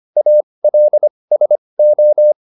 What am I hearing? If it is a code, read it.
Morse code, heard as a keyed tone, about 25 wpm, spaced as for 20 wpm, 600 Hz. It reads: ALSO
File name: SS206 20